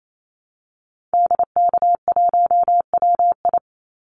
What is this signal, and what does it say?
Morse code, heard as a keyed tone, about 28 wpm, 700 Hz. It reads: DK1WI
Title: Call Signs Repeated 2x - 28wpm